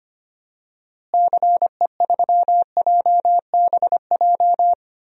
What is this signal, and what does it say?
Morse code, heard as a keyed tone, about 25 wpm, 700 Hz. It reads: CE3JBJ
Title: DXCC-100 Call Signs - 6 Characters - Repeated 2x - 25wpm